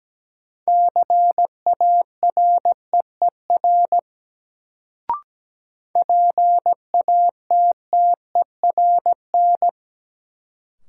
Morse code, heard as a keyed tone, about 17 wpm, 700 Hz. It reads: CAREER PATTERN
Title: Sentences from Top 1000 Words - New Word Drill - 17wpm